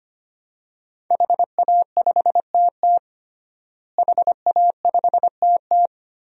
Morse code, heard as a keyed tone, about 25 wpm, 700 Hz. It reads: HA5TT HA5TT
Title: DXCC-100 Call Signs - 5 Characters - Repeated 3x - 25wpm